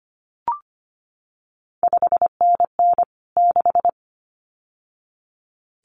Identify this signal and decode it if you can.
Morse code, heard as a keyed tone, about 25 wpm, 700 Hz. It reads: 5NN 6
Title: CQ WW DX Contest Exchange Encore - 25wpm